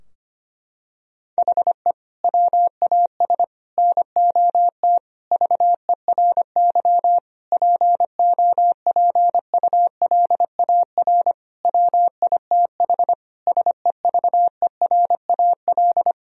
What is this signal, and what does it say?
Morse code, heard as a keyed tone, about 25 wpm, 700 Hz. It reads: HE WAS NOT VERY POPULAR WITH SEVERAL
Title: Sentences from Top 1000 Words - Review Part 3 - 25wpm